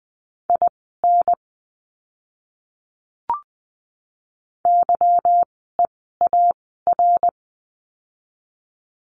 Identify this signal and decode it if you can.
Morse code, heard as a keyed tone, about 20 wpm, spaced as for 14 wpm, 700 Hz. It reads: IN YEAR
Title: 244 Lesson 39f - Letter z - Words 20wpm